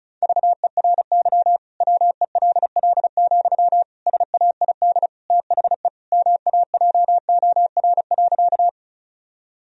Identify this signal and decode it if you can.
Morse code, heard as a keyed tone, about 34 wpm, 700 Hz. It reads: VERY WELL, SAID THE MAJOR.
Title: Sentences from Top 1000 Words - Review Part 4 - 35wpm